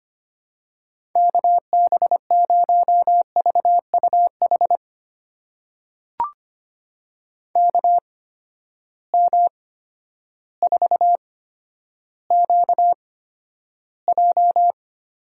Morse code, heard as a keyed tone, about 25 wpm, spaced as for 11 wpm, 700 Hz. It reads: KB0VUH KM4QJ
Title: USA Call Signs - 6 Characters - 8x Character Spacing - 25wpm